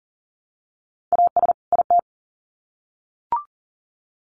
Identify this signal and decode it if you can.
Morse code, heard as a keyed tone, about 40 wpm, 700 Hz. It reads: AS IT
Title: Sets of 2 Words 40wpm